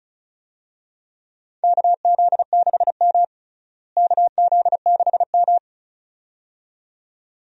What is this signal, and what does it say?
Morse code, heard as a keyed tone, about 34 wpm, 700 Hz. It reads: KZ6M KZ6M
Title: USA Call Signs - 4 Characters - Repeated 2x - 35wpm